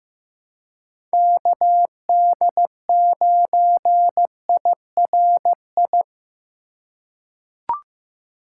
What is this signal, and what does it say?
Morse code, heard as a keyed tone, about 15 wpm, 700 Hz. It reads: KD9IRI
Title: USA Call Signs - 6 Characters - 4x Character Spacing - 15wpm